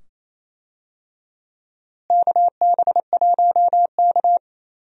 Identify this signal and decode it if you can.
Morse code, heard as a keyed tone, about 28 wpm, 700 Hz. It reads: KB1K
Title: USA Call Signs - 4 Characters - Repeated 2x - 28wpm